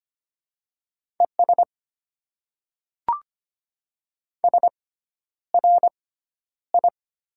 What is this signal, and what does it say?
Morse code, heard as a keyed tone, about 25 wpm, spaced as for 11 wpm, 690 Hz. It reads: ES SRI
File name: Top 90 QSO Elements - 6x Character Spacing - 25wpm